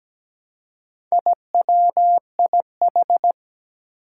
Morse code, heard as a keyed tone, about 17 wpm, 700 Hz. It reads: IWIH
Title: Four Challenging Letter-Number Combinations 17wpm